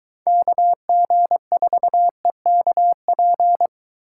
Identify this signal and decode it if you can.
Morse code, heard as a keyed tone, about 23 wpm, 700 Hz. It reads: KG4EKP